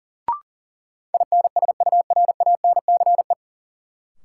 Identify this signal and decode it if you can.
Morse code, heard as a keyed tone, about 40 wpm, 700 Hz. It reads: INSURANCE
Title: Sentences from Top 1100 Words - New Word Drill - 40wpm